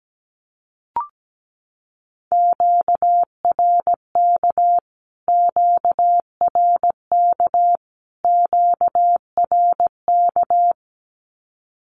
Morse code, heard as a keyed tone, about 17 wpm, 700 Hz. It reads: QRK QRK QRK
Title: Q-Codes Repeated 3x 17wpm